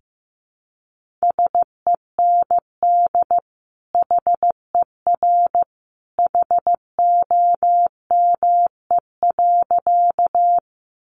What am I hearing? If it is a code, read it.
Morse code, heard as a keyed tone, about 15 wpm, 700 Hz. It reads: SEND HER HOME.
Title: Sentences from Top 1500 Words 15wpm